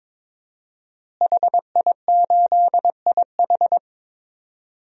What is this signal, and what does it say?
Morse code, heard as a keyed tone, about 22 wpm, 700 Hz. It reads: HI8IH
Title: DXCC-100 Call Signs - 5 Characters - Repeated 2x - 22wpm